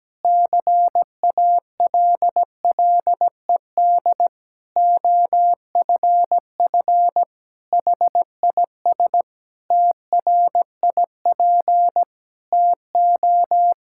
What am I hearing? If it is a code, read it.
Morse code, heard as a keyed tone, about 17 wpm, 700 Hz. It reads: CALLED OFF HIS TRIP TO
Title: Pangram Challenge v2 - 17wpm